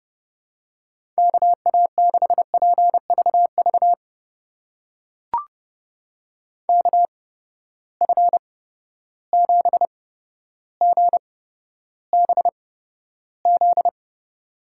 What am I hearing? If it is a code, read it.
Morse code, heard as a keyed tone, about 30 wpm, spaced as for 13 wpm, 700 Hz. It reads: KA6PVV KF7GBZ
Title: USA Call Signs - 6 Characters - 8x Character Spacing - 30wpm